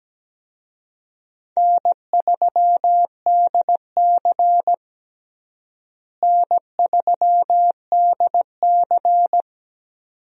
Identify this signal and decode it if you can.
Morse code, heard as a keyed tone, about 17 wpm, 700 Hz. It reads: N3DC N3DC
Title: USA Call Signs - 4 Characters - Repeated 2x - 17wpm